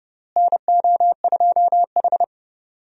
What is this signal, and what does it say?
Morse code, heard as a keyed tone, about 30 wpm, 700 Hz. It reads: NO2H
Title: USA Call Signs - 4 Characters - Repeated 2x - 30wpm